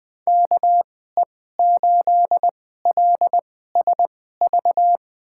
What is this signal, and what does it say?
Morse code, heard as a keyed tone, about 21 wpm, spaced as for 15 wpm, 700 Hz. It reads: KE8LSV
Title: USA Call Signs - 6 Characters - 2x Character Spacing - 20wpm